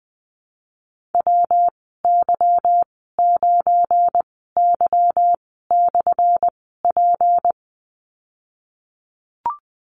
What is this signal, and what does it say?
Morse code, heard as a keyed tone, about 20 wpm, spaced as for 15 wpm, 700 Hz. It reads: WY9Y/P
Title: 179 Lesson 29g - Slash - Call Signs 20wpm